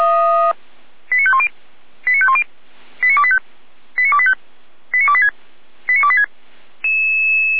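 Five-tone selective call (ZVEI), heard as quick runs of five tones, tones of about 70 ms. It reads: 89510 89510 89285 89285 89286 89286
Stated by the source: ZVEI five-tone selective call received on SDR.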